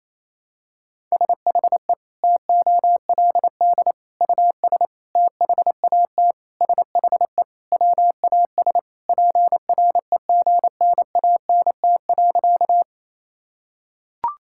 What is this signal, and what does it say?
Morse code, heard as a keyed tone, about 28 wpm, 700 Hz. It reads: SHE TOLD US THAT SHE WAS PREGNANT.
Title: Sentences from Top 1000 Words - Review Part 6 - 28wpm